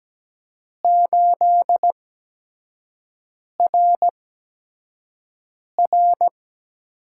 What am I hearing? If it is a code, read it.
Morse code, heard as a keyed tone, about 17 wpm, 700 Hz. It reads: 8 R R
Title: USA Call Signs - 8x Character Spacing - 17wpm